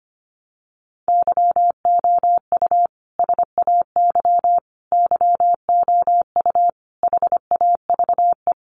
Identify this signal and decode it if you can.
Morse code, heard as a keyed tone, about 25 wpm, 700 Hz. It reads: YOU SAY YOU HAVE
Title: Sentences from Top 700 Words 25wpm